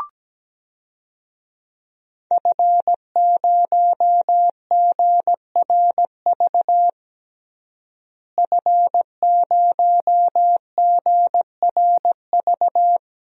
Morse code, heard as a keyed tone, about 17 wpm, 700 Hz. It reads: F0GRV F0GRV
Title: DXCC-100 Call Signs - 5 Characters - Repeated 2x - 17wpm